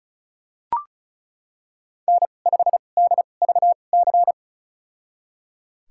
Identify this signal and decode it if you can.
Morse code, heard as a keyed tone, about 34 wpm, spaced as for 25 wpm, 700 Hz. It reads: N5DVC
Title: USA Call Signs - 5 Characters - 2x Character Spacing - 35wpm